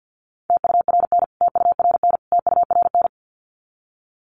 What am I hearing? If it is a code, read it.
Morse code, heard as a keyed tone, about 48 wpm, 700 Hz. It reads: TURN TURN TURN
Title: Top 200 Words Repeated 50wpm